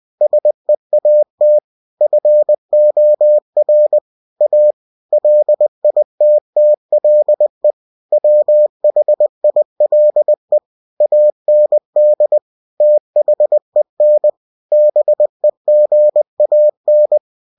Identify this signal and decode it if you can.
Morse code, heard as a keyed tone, about 20 wpm, 600 Hz. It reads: SEAT FOR A LITTLE WHILE AND THEN BEGAN
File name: cw-books / winnie-the-pooh / winnie-20wpm-page-0106